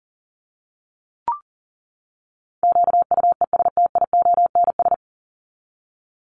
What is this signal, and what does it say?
Morse code, heard as a keyed tone, about 39 wpm, 700 Hz. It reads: QUESTIONS
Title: ES:S Suffix - Top 100 Words with Base Words 40wpm